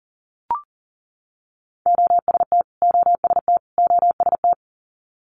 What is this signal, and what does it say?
Morse code, heard as a keyed tone, about 40 wpm, 700 Hz. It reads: OST OST OST
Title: Multi-letter Phonemes Repeated 3x 40wpm